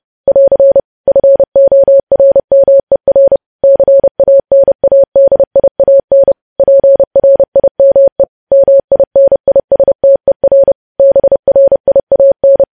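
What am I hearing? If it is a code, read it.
Morse code, heard as a keyed tone, about 30 wpm, 550 Hz. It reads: <AR> FORMER CANADIAN PRIME MINISTER BRIAN
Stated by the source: Morse news headlines at 30 wpm.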